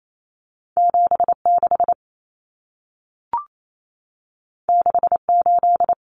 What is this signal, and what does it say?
Morse code, heard as a keyed tone, about 28 wpm, 700 Hz. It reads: 76 68
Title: Numbers - 2-Digit 28wpm